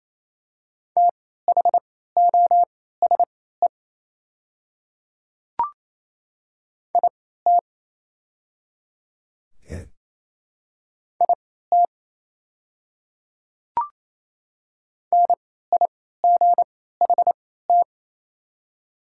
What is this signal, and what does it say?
Morse code, heard as a keyed tone, about 28 wpm, spaced as for 16 wpm, 700 Hz. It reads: THOSE IT IT NIGHT
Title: 3x Character Spacing - Top 300 Words - 28wpm